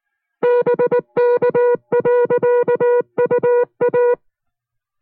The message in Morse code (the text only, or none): BK.UA